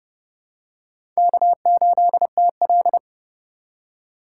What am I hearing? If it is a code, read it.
Morse code, heard as a keyed tone, about 32 wpm, 700 Hz. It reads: K8TL
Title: USA Call Signs - 3x Character Spacing - 30wpm